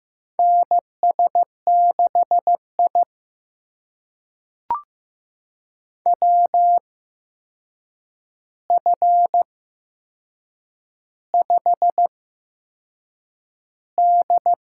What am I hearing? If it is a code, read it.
Morse code, heard as a keyed tone, about 15 wpm, spaced as for 6 wpm, 700 Hz. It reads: NS6I WF5D
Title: USA Call Signs - 4 Characters - 8x Character Spacing - 15wpm